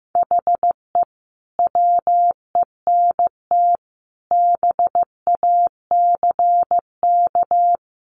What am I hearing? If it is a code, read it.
Morse code, heard as a keyed tone, about 15 wpm, 700 Hz. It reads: HE WENT BACK
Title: Sentences from Top 1100 Words 15wpm